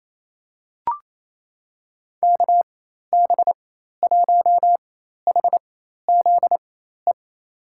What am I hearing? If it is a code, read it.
Morse code, heard as a keyed tone, about 27 wpm, spaced as for 13 wpm, 700 Hz. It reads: KB1HZE
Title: USA Call Signs - 6 Characters - 4x Character Spacing - 28wpm